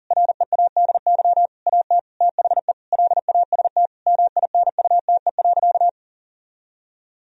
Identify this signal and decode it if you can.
Morse code, heard as a keyed tone, about 40 wpm, 700 Hz. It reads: READY AT THE LAST MINUTE.
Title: Sentences from Top 1000 Words - Review Part 2 - 40wpm